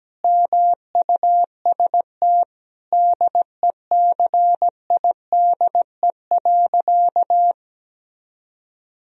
Morse code, heard as a keyed tone, about 17 wpm, 700 Hz. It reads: MUST DECIDE.
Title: Sentences from Top 1000 Words - Review Part 3 - 17wpm